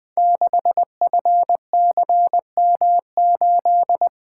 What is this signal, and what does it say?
Morse code, heard as a keyed tone, about 20 wpm, 700 Hz. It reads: 6FCM8
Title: Code Groups - 5 Letter:Numbers - 20wpm